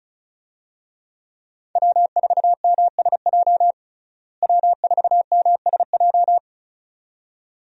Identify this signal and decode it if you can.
Morse code, heard as a keyed tone, about 36 wpm, 700 Hz. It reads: W4MSJ W4MSJ
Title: USA Call Signs - 5 Characters - Repeated 2x - 35wpm